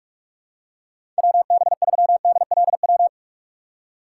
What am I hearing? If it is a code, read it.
Morse code, heard as a keyed tone, about 46 wpm, 690 Hz. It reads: WB3DLW
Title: USA Call Signs - 6 Characters - 3x Character Spacing - 45wpm